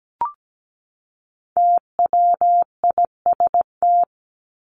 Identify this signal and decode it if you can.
Morse code, heard as a keyed tone, about 17 wpm, 700 Hz. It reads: TWIST
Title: Top 1751-2000 Words 17wpm